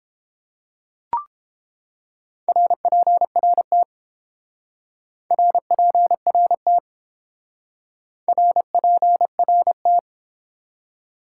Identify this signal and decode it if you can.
Morse code, heard as a keyed tone, about 30 wpm, 700 Hz. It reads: RPRT RPRT RPRT